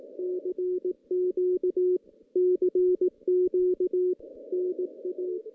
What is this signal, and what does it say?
Morse code, heard as a keyed tone, about 18 wpm, 370 Hz. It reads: CQ CQ NA
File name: 2016 JAN CW NAQP / 2016 Jan naqp k5ot - 0-8